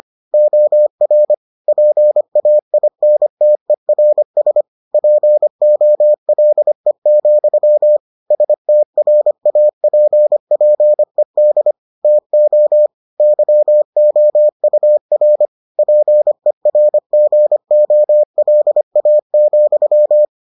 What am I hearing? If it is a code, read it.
Morse code, heard as a keyed tone, about 25 wpm, 600 Hz. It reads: OR PAINTERS POLE, STRAPPED TO YOUR PERGOLA,